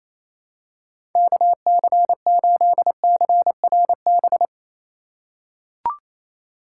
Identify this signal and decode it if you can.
Morse code, heard as a keyed tone, about 28 wpm, 700 Hz. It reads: KC8CRB
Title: USA Call Signs - 6 Characters - 2x Character Spacing - 28wpm